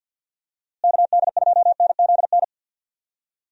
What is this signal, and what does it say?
Morse code, heard as a keyed tone, about 51 wpm, 700 Hz. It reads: KD2NZN
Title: USA Call Signs - 6 Characters - Repeated 2x - 50wpm